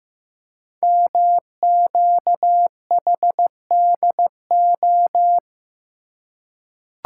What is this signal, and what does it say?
Morse code, heard as a keyed tone, about 15 wpm, 700 Hz. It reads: MQHDO
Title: Code Groups - 5 Letters - 15wpm